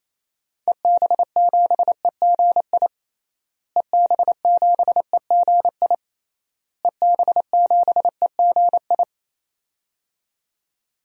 Morse code, heard as a keyed tone, about 28 wpm, 700 Hz. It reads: EB7EGI EB7EGI EB7EGI
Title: DXCC-100 Call Signs - 6 Characters - Repeated 3x - 28wpm